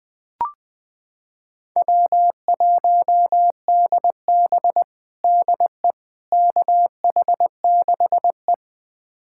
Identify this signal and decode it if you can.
Morse code, heard as a keyed tone, about 20 wpm, 700 Hz. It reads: W1DB DE KH6E
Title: USA Callsign DE Callsign - 4 Characters - 20wpm